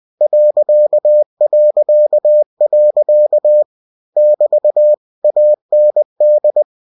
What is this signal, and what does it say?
Morse code, heard as a keyed tone, about 20 wpm, 600 Hz. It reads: ... <BT> AND
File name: cw-books / winnie-the-pooh / winnie-20wpm-page-0019